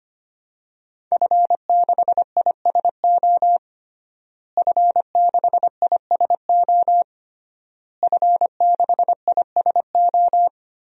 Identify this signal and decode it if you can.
Morse code, heard as a keyed tone, about 25 wpm, 700 Hz. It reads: F6ISO F6ISO F6ISO
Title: DXCC-100 Call Signs - 5 Characters - Repeated 3x - 25wpm